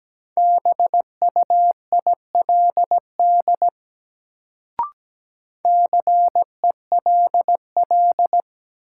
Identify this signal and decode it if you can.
Morse code, heard as a keyed tone, about 17 wpm, 700 Hz. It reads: BUILD CELL
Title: Sentences from Top 1100 Words - New Word Drill Alt - 17wpm